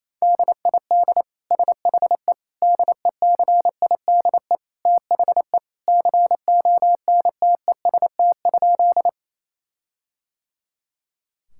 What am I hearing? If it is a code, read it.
Morse code, heard as a keyed tone, about 27 wpm, 700 Hz. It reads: DID SHE DECIDE THE CONTEST?
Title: Sentences from Top 1600 Words 28wpm